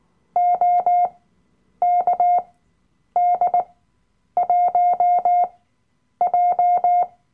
Morse code, heard as a keyed tone, about 19 wpm, spaced as for 9 wpm, 710 Hz. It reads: OKD1J
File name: Workshop Dateien / CW-Online-Kurs / Lektion 007